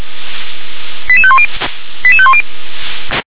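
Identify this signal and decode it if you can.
Five-tone selective call (ZVEI), heard as quick runs of five tones, tones of about 70 ms. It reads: 88510 88510
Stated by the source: Five-tone ZVEI selective call, received via SDR.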